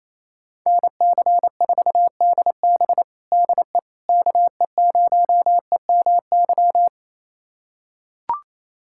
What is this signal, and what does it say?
Morse code, heard as a keyed tone, about 28 wpm, 700 Hz. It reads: NC4DB DE KE0EMY
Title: USA Callsign DE Callsign 28wpm